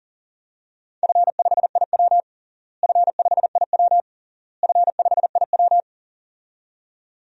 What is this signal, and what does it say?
Morse code, heard as a keyed tone, about 37 wpm, 700 Hz. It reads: F5IW F5IW F5IW
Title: DXCC-100 Call Signs - 4 Characters - Repeated 3x - 40wpm